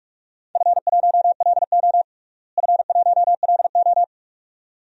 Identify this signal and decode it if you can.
Morse code, heard as a keyed tone, about 42 wpm, 700 Hz. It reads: F1LO F1LO
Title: DXCC-100 Call Signs - 4 Characters - Repeated 2x - 45wpm